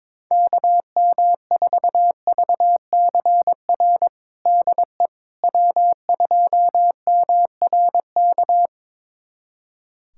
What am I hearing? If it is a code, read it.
Morse code, heard as a keyed tone, about 23 wpm, 700 Hz. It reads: KM4VCR DE W2MRK
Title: USA Callsign DE Callsign 22wpm